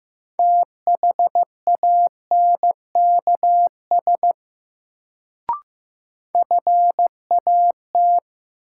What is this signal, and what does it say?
Morse code, heard as a keyed tone, about 15 wpm, 700 Hz. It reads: THANKS FAT
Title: Sentences from Top 600 Words - New Word Drill - 15wpm